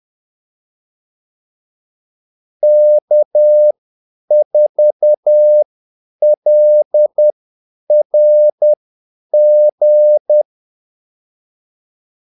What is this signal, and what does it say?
Morse code, heard as a keyed tone, about 10 wpm, spaced as for 8 wpm, 600 Hz. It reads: K4LRG
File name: Callsigns-10-8wpm